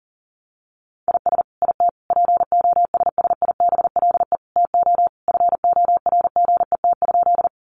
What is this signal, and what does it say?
Morse code, heard as a keyed tone, about 40 wpm, 700 Hz. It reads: IS IT POSSIBLE TO FORGET?
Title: Sentences from Top 1200 Words 40wpm